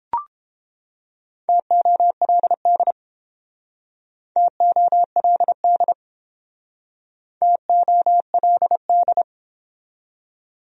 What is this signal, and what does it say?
Morse code, heard as a keyed tone, about 31 wpm, 700 Hz. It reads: TOLD TOLD TOLD